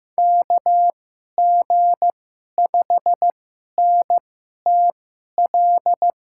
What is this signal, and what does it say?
Morse code, heard as a keyed tone, about 15 wpm, spaced as for 11 wpm, 700 Hz. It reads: KG5NTL
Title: USA Call Signs - 6 Characters - 2x Character Spacing - 15wpm